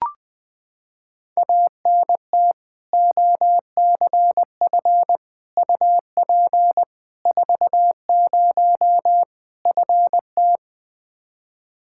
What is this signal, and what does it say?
Morse code, heard as a keyed tone, about 20 wpm, 700 Hz. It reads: ANT OCF UP 40 FT
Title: K1USN SST QSO Practice 20wpm